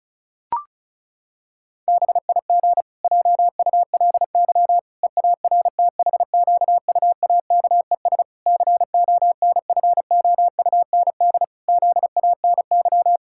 Morse code, heard as a keyed tone, about 36 wpm, 700 Hz. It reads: BIG JULY EARTHQUAKES CONFOUND ZANY
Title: Pangram Challenge v2 - 35wpm